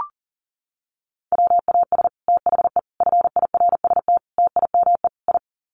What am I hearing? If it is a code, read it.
Morse code, heard as a keyed tone, about 40 wpm, 700 Hz. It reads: WAS THE FIRST TIME I
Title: Sets of 5 Words Encore 40wpm